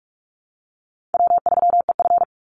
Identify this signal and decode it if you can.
Morse code, heard as a keyed tone, about 44 wpm, 700 Hz. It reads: W3EF
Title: Easy Call Signs Repeated 3x - 45wpm